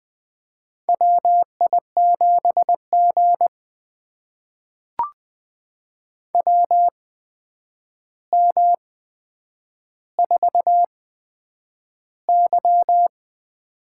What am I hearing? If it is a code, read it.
Morse code, heard as a keyed tone, about 20 wpm, spaced as for 9 wpm, 700 Hz. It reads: WI7G WM4Y
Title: USA Call Signs - 4 Characters - 8x Character Spacing - 20wpm